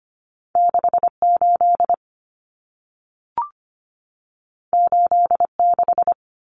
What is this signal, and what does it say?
Morse code, heard as a keyed tone, about 25 wpm, 700 Hz. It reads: 68 86
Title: Common Cut Numbers - 2-Digit 25wpm